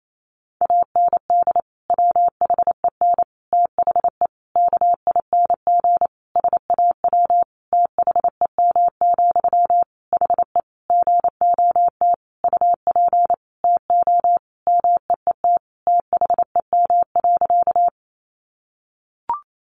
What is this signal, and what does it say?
Morse code, heard as a keyed tone, about 28 wpm, 700 Hz. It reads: AND WHEN THE KING SAW THEM, HE GOT UP TO MEET THEM.
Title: Sentences from Top 1100 Words 28wpm